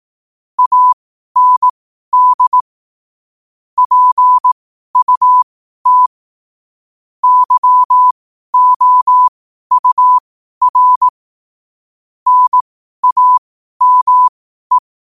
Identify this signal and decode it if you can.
Morse code, heard as a keyed tone, about 18 wpm, spaced as for 12 wpm, 1000 Hz. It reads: AND PUT YOUR NAME
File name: 12wpm